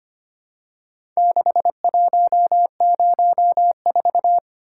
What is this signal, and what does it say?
Morse code, heard as a keyed tone, about 25 wpm, 700 Hz. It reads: 6104
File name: practice / 25wpm / Four Letter-Number Combinations - 25wpm